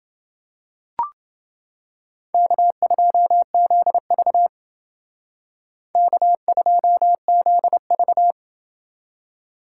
Morse code, heard as a keyed tone, about 30 wpm, 700 Hz. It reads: K2ZV K2ZV